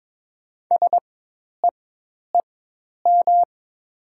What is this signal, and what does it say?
Morse code, heard as a keyed tone, about 23 wpm, spaced as for 10 wpm, 700 Hz. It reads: SEEM
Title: 4x Character Spacing - Top 300 Words - 22wpm